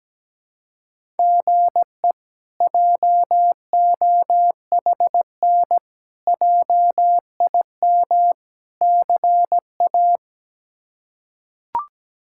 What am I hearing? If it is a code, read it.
Morse code, heard as a keyed tone, about 17 wpm, 700 Hz. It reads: GE JOHN JIM CA
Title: K1USN SST QSO Practice - 17wpm